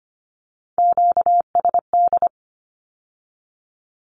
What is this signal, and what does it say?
Morse code, heard as a keyed tone, about 25 wpm, 700 Hz. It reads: QSD
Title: Q-Codes with Spoken Meaning - 25wpm